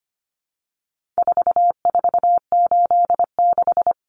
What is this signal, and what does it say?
Morse code, heard as a keyed tone, about 25 wpm, 700 Hz. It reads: 4486
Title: Numbers - 4-Digit 25wpm